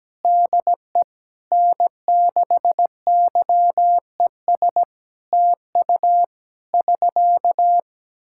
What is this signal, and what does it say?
Morse code, heard as a keyed tone, about 17 wpm, 700 Hz. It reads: DE N6YES TU <SK>
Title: Practice QSO - Exchange 4 - Ending - 17wpm